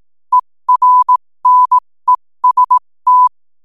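Morse code, heard as a keyed tone, about 18 wpm, spaced as for 15 wpm, 1000 Hz. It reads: ERNEST
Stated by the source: 15 wpm Morse code from RNZN Communicators Association.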